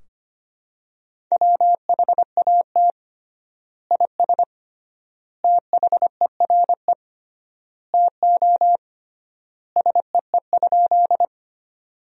Morse code, heard as a keyed tone, about 25 wpm, 700 Hz. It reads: WHAT IS THERE TO SEE?
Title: 3x Word-Spacing - Sentences from Top 100 Words - 25wpm